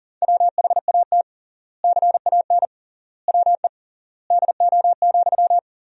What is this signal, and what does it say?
Morse code, heard as a keyed tone, about 37 wpm, 700 Hz. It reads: WHAT CAN WE DO,
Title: 3x Word-Spacing - Sentences from Top 100 Words - 40wpm